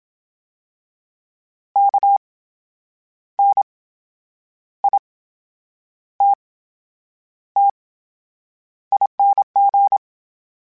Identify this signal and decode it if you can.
Morse code, heard as a keyed tone, about 27 wpm, spaced as for 8 wpm, 800 Hz. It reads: KNITTING
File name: Special INGs / Special INGs-mp3s / VerySlowToFaster Ing @25